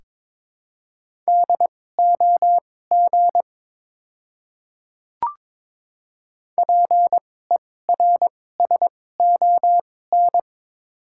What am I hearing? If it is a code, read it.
Morse code, heard as a keyed tone, about 22 wpm, spaced as for 16 wpm, 700 Hz. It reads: DOG PERSON
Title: 2x Character Spacing - Top 500 Words - 22wpm